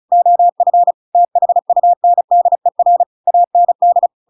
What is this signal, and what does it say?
Morse code, heard as a keyed tone, about 35 wpm, 700 Hz. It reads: OF THUNDER AND
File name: Pinocchio / Pinocchio ch6 35 35 ik1hge